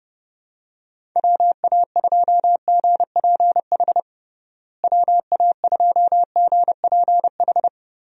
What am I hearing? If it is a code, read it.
Morse code, heard as a keyed tone, about 32 wpm, 700 Hz. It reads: WA2GPH WA2GPH
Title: USA Call Signs - 6 Characters - Repeated 3x - 30wpm